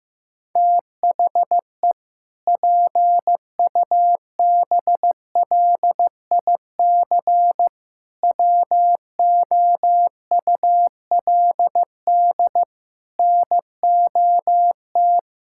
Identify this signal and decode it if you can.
Morse code, heard as a keyed tone, about 15 wpm, 700 Hz. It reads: THE PUBLIC WOULD NOT
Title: Sentences from Top 1600 Words 15wpm